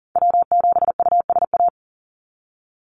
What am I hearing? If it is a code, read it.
Morse code, heard as a keyed tone, about 40 wpm, 700 Hz. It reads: W7USA
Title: NA SOTA Activator Callsigns 40wpm